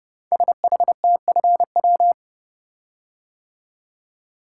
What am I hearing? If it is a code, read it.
Morse code, heard as a keyed tone, about 30 wpm, 700 Hz. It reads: SHTFW
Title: Code Groups - 5 Letters - 30wpm